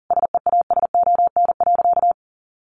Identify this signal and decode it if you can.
Morse code, heard as a keyed tone, about 40 wpm, 700 Hz. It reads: SEASON.
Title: Sentences from Top 1400 Words 40wpm